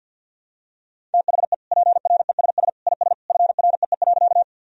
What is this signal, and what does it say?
Morse code, heard as a keyed tone, about 51 wpm, 700 Hz. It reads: THE PRESS IS FREE.
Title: Sentences from Top 1000 Words - Review Part 1 - 50wpm